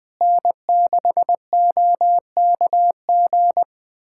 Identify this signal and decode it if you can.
Morse code, heard as a keyed tone, about 21 wpm, 700 Hz. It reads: N6OKG
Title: USA Callsign DE Callsign - 5 Characters - 20wpm